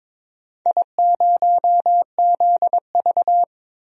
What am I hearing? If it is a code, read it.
Morse code, heard as a keyed tone, about 22 wpm, 700 Hz. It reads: I0ZV
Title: DXCC-100 Call Signs - 4 Characters - Repeated 2x - 22wpm